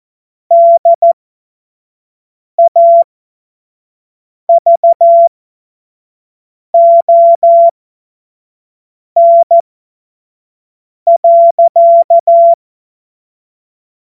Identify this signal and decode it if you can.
Morse code, heard as a keyed tone, about 14 wpm, spaced as for 5 wpm, 680 Hz. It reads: DAVON.
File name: Morsekurs / Witze / Witz 27 Februar 2024 25 F70